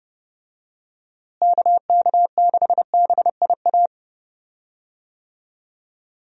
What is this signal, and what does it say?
Morse code, heard as a keyed tone, about 32 wpm, 700 Hz. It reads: KK6BIA
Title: USA Call Signs - 6 Characters - Repeated 3x - 30wpm